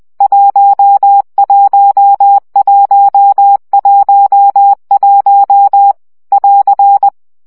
Morse code, heard as a keyed tone, about 20 wpm, 800 Hz. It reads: 11111 <AR>
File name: CW gyakorlas 1 17 baud